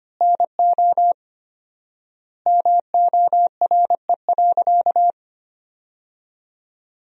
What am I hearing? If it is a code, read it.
Morse code, heard as a keyed tone, about 25 wpm, 700 Hz. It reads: NO MORE.
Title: 4x Word-Spacing - Sentences from Top 300 Words - 25wpm